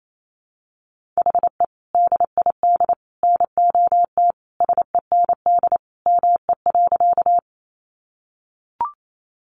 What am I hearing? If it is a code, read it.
Morse code, heard as a keyed tone, about 28 wpm, 700 Hz. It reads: HE DID NOT SEND ME.
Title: Sentences from Top 1500 Words 28wpm